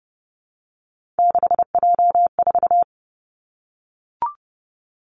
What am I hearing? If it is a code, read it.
Morse code, heard as a keyed tone, about 30 wpm, 700 Hz. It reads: 6J4
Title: Three Challenging Letter-Number Combinations 30wpm